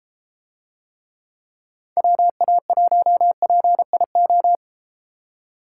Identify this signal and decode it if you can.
Morse code, heard as a keyed tone, about 33 wpm, 700 Hz. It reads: WA1PIO